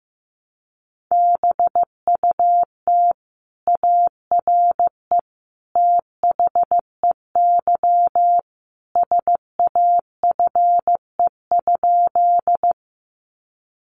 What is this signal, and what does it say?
Morse code, heard as a keyed tone, about 15 wpm, 700 Hz. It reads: BUT ARE THEY SAFE?
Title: Sentences from Top 900 Words 15wpm